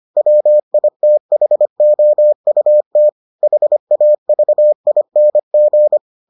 Morse code, heard as a keyed tone, about 25 wpm, 600 Hz. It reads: WITHOUT HAVING